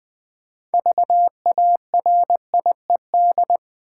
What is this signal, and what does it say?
Morse code, heard as a keyed tone, about 20 wpm, 700 Hz. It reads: VARIED
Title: 6 Letter Words - First 3 Letters Given - 20wpm